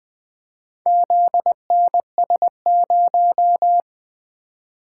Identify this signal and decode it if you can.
Morse code, heard as a keyed tone, about 23 wpm, 700 Hz. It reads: ZNS0
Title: Speed-Racing - Four Letter-Number Combinations - 20wpm